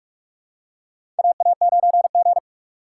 Morse code, heard as a keyed tone, about 43 wpm, 700 Hz. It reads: AA9G
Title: USA Call Signs - 4 Characters - 2x Character Spacing - 45wpm